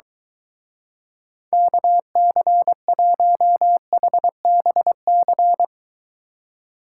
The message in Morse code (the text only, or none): KC1HBC